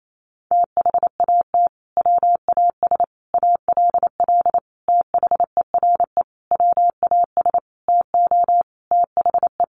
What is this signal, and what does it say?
Morse code, heard as a keyed tone, about 28 wpm, 700 Hz. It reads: THAT WAS ALL THERE WAS TO THE
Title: Sentences from Top 1100 Words 28wpm